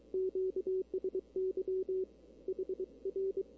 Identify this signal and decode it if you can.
Morse code, heard as a keyed tone, about 24 wpm, 380 Hz. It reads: QSY HR
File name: CW / 2005-k5tr arrl cw dx -16:00